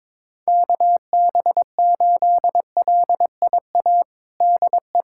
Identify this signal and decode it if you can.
Morse code, heard as a keyed tone, about 21 wpm, 700 Hz. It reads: KB8LIA DE
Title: USA Callsign DE Callsign - 6 Characters - 22wpm